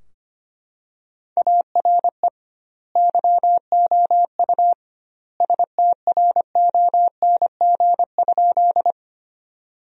Morse code, heard as a keyed tone, about 25 wpm, 700 Hz. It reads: ARE YOU STRONG?
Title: 2x Word-Spacing - Sentences from Top 500 Words - 25wpm